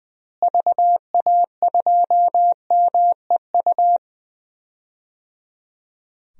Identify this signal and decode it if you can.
Morse code, heard as a keyed tone, about 20 wpm, 700 Hz. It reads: VA2MEU
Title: DXCC-100 Call Signs - 6 Characters - Repeated 3x - 20wpm